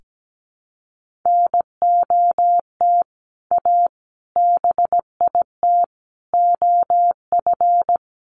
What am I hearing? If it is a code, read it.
Morse code, heard as a keyed tone, about 17 wpm, 700 Hz. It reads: NOT A BIT OF
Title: Sentences from Top 1400 Words 17wpm